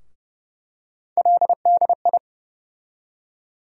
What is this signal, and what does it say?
Morse code, heard as a keyed tone, about 30 wpm, 700 Hz. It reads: LDI
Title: USA Call Sign Postfixes - 3 Letters 30wpm